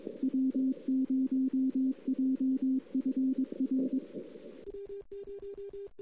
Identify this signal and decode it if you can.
Morse code, heard as a keyed tone, about 22 wpm, 260 Hz. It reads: W0JFR
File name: CW / 2008-ss-cw-k5tr-op-n5rz -03:00